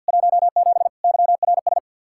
Morse code, heard as a keyed tone, about 66 wpm, spaced as for 58 wpm, 700 Hz. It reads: 17 YRS
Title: Speed-Racing Common QSO Phrases - 50wpm